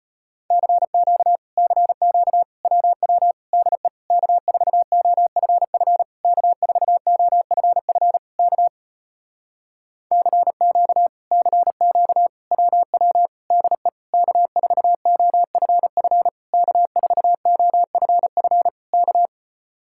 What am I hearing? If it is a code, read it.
Morse code, heard as a keyed tone, about 37 wpm, 700 Hz. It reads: CQ CQ WW DE K4OFF K4OFF K CQ CQ WW DE K4OFF K4OFF K